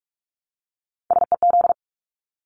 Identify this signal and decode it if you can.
Morse code, heard as a keyed tone, about 44 wpm, 700 Hz. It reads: SEZ
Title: Abbreviations 45wpm